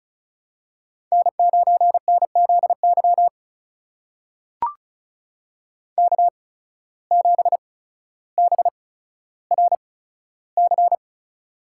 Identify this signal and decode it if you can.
Morse code, heard as a keyed tone, about 36 wpm, spaced as for 15 wpm, 700 Hz. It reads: N9NZY K7BRC
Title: USA Call Signs - 8x Character Spacing - 35wpm